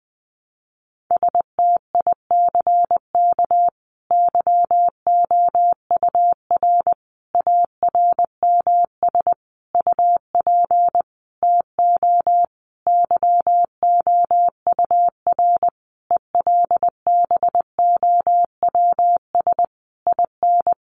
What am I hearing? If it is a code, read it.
Morse code, heard as a keyed tone, about 20 wpm, 700 Hz. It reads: STICK YOUR ARMS UP TO YOUR ELBOWS IN
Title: A Sound Of Thunder 20wpm